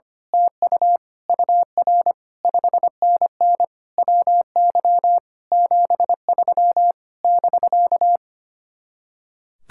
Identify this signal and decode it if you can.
Morse code, heard as a keyed tone, about 24 wpm, 700 Hz. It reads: TU UR 5NN WY 73 <BK>
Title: Speed-Racing Common QSO Phrases - 25wpm